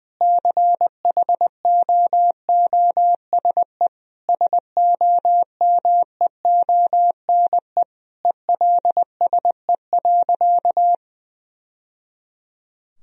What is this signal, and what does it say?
Morse code, heard as a keyed tone, about 21 wpm, 700 Hz. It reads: CHOOSE SOMEONE ELSE.
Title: Sentences from Top 1000 Words - Review Part 4 - 20wpm